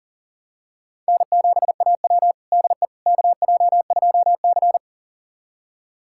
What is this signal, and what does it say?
Morse code, heard as a keyed tone, about 40 wpm, 700 Hz. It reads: N7AW DE KJ2C
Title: USA Callsign DE Callsign - 4 Characters - 40wpm